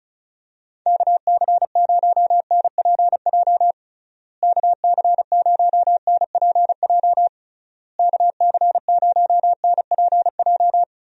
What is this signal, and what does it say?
Morse code, heard as a keyed tone, about 34 wpm, 700 Hz. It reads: KC0NPJ KC0NPJ KC0NPJ
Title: USA Call Signs - 6 Characters - Repeated 3x - 35wpm